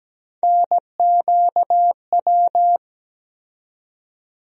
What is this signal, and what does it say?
Morse code, heard as a keyed tone, about 17 wpm, 700 Hz. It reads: NQW
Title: USA Call Sign Postfixes - 3 Letters 17wpm